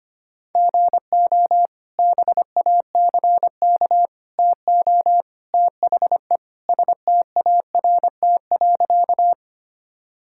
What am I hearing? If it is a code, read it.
Morse code, heard as a keyed tone, about 25 wpm, 700 Hz. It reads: GO BACK TO THE START.
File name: practice / 25wpm / Sentences with 3-Word Phrases - 25wpm